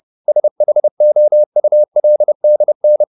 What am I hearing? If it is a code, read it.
Morse code, heard as a keyed tone, about 30 wpm, 600 Hz. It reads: SHOULDN
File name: the old man and the sea / the old man and the sea 1 30wpm